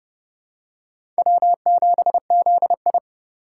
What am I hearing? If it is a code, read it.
Morse code, heard as a keyed tone, about 30 wpm, 700 Hz. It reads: W7ZI
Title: USA Call Signs - 4 Characters - 4x Character Spacing - 30wpm